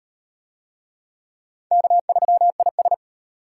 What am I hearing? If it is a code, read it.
Morse code, heard as a keyed tone, about 38 wpm, 700 Hz. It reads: K3IS